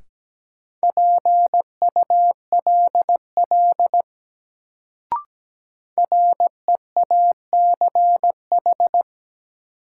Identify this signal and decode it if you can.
Morse code, heard as a keyed tone, about 17 wpm, 700 Hz. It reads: PULL REACH
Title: Sentences from Top 400 Words - New Word Drill - 17wpm